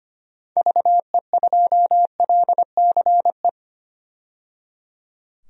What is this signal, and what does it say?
Morse code, heard as a keyed tone, about 25 wpm, 700 Hz. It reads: VE2LCE
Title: DXCC-100 Call Signs - 6 Characters - 25wpm